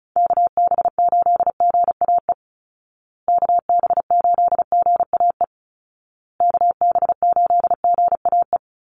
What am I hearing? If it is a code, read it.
Morse code, heard as a keyed tone, about 34 wpm, 700 Hz. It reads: KB8GAE KB8GAE KB8GAE
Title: Call Signs Repeated 2x - 35wpm